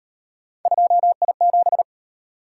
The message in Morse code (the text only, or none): 2I7